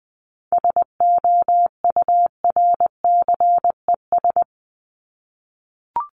Morse code, heard as a keyed tone, about 20 wpm, 700 Hz. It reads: SOURCES
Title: ES:S Suffix - Top 100 Words with Base Words 20wpm